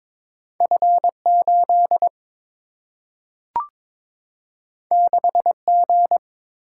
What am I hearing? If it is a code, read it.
Morse code, heard as a keyed tone, about 22 wpm, 700 Hz. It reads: F 8 6 G
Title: Single-Letter Single-Number 22wpm